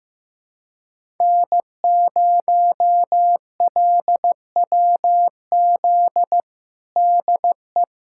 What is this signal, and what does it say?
Morse code, heard as a keyed tone, about 16 wpm, 700 Hz. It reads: N0LWZ DE
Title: USA Callsign DE Callsign - 5 Characters - 15wpm